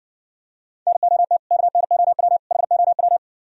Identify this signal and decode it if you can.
Morse code, heard as a keyed tone, about 61 wpm, 700 Hz. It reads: NOT KNOW HOW